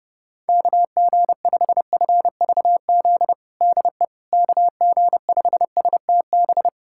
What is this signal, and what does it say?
Morse code, heard as a keyed tone, about 30 wpm, 700 Hz. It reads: KG5FVZ DE KG5STB
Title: USA Callsign DE Callsign - 6 Characters - 30wpm